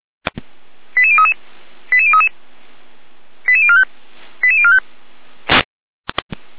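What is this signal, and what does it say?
Five-tone selective call (ZVEI), heard as quick runs of five tones, tones of about 70 ms. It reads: 88033 88033 88045 88045
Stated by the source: ZVEI five-tone call 88127, received on SDR, recorded by feuerwehr-orlamuende.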